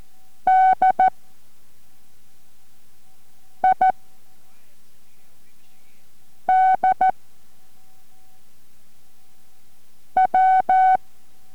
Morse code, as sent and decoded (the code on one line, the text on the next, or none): -.. / .. / -.. / .--
D I D W